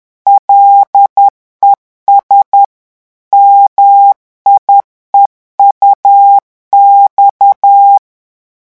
Morse code, heard as a keyed tone, about 10 wpm, 800 Hz. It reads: LES MIEUX
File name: morse mp3 / revision9 citations humoristiques vitesse 10 mots minute